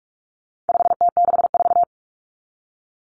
Morse code, heard as a keyed tone, about 44 wpm, 700 Hz. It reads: 5T64
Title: Common Cut Numbers - 4-Digit 45wpm